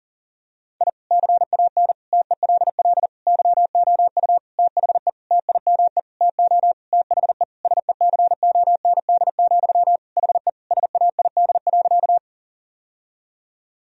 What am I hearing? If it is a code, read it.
Morse code, heard as a keyed tone, about 40 wpm, 700 Hz. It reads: I CAN TELL YOU THE TIME TO THE SECOND, HE SAID.
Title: Sentences from Top 1000 Words - Review Part 1 - 40wpm